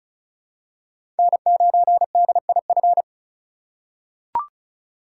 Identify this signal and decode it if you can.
Morse code, heard as a keyed tone, about 36 wpm, 700 Hz. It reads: N9DIF
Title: USA Call Signs - 5 Characters - 4x Character Spacing - 35wpm